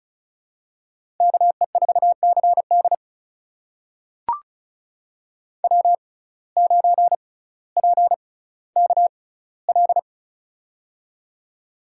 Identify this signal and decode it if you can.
Morse code, heard as a keyed tone, about 35 wpm, spaced as for 18 wpm, 700 Hz. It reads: KE4CD W9PKL
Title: USA Call Signs - 5 Characters - 6x Character Spacing - 35wpm